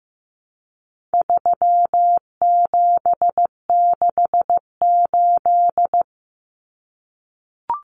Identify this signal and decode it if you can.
Morse code, heard as a keyed tone, about 15 wpm, 700 Hz. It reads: 3768
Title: Numbers - 4-Digit 15wpm